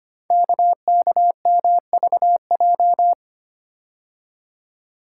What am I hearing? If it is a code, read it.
Morse code, heard as a keyed tone, about 25 wpm, 700 Hz. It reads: KKMVJ
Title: Code Groups - 5 Letters - 25wpm